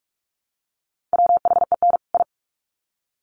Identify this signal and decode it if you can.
Morse code, heard as a keyed tone, about 44 wpm, 700 Hz. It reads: WHEN I
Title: Sets of 2 Words 45wpm